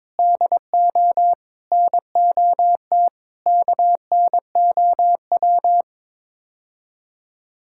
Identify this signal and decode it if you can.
Morse code, heard as a keyed tone, about 21 wpm, 700 Hz. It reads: DO NOT KNOW
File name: practice / 22wpm / Build A Phrase - 5-Word Phrases - 22wpm